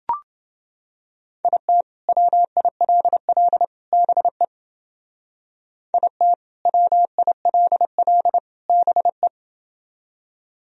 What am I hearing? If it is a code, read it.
Morse code, heard as a keyed tone, about 29 wpm, 700 Hz. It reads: IT WILL BE IT WILL BE